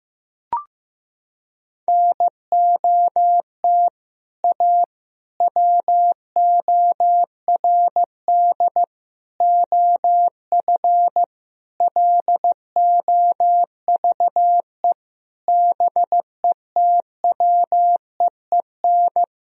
Morse code, heard as a keyed tone, about 15 wpm, 700 Hz. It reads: NOT A WORD OF LOVE BETWEEN
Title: Sentences from Top 1000 Words - Review Part 1 - 15wpm